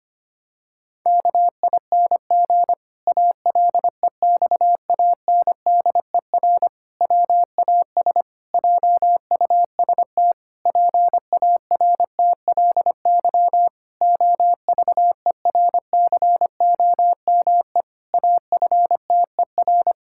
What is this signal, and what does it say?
Morse code, heard as a keyed tone, about 25 wpm, 700 Hz. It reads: KING ALEXANDER WAS JUST PARTLY OVERCOME AFTER
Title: Pangram Challenge v2 - 25wpm